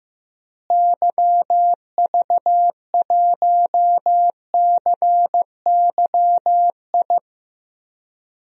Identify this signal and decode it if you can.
Morse code, heard as a keyed tone, about 15 wpm, 700 Hz. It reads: YV1CYI
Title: DXCC-100 Call Signs - 6 Characters - Repeated 2x - 15wpm